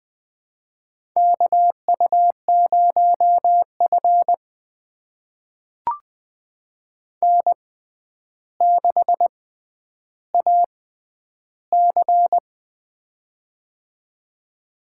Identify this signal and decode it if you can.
Morse code, heard as a keyed tone, about 20 wpm, spaced as for 10 wpm, 700 Hz. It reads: KU0F N6AC
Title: USA Call Signs - 4 Characters - 6x Character Spacing - 20wpm